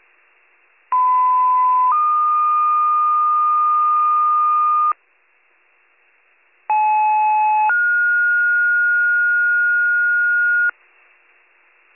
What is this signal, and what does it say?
Two-tone page, heard as a one-second tone, then a three-second tone, about 1010 then 1220 Hz, about 860 then 1430 Hz. It is heard from 0.9 to 10.7 s.